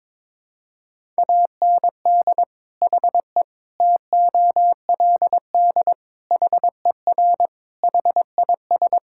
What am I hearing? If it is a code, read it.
Morse code, heard as a keyed tone, about 22 wpm, 700 Hz. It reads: AND HE TOLD HER HIS
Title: Sentences from Top 1000 Words - Review Part 5 - 22wpm